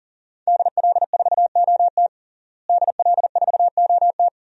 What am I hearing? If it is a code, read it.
Morse code, heard as a keyed tone, about 40 wpm, 700 Hz. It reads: DL4OT DL4OT
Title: USA Call Signs - 5 Characters - Repeated 3x - 40wpm